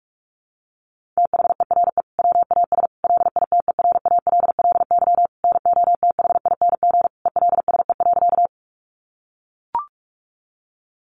Morse code, heard as a keyed tone, about 44 wpm, 700 Hz. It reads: THERE WAS LITERALLY NOTHING ELSE.
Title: Sentences from Top 1400 Words 45wpm